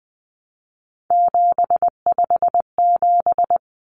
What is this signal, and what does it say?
Morse code, heard as a keyed tone, about 20 wpm, 700 Hz. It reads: 757
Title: Common Cut Numbers - 3-Digit 20wpm